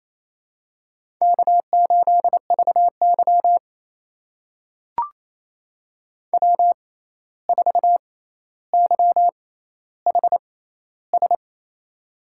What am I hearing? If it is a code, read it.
Morse code, heard as a keyed tone, about 28 wpm, spaced as for 13 wpm, 700 Hz. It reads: K8VY W4YHS
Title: USA Call Signs - 6x Character Spacing - 28wpm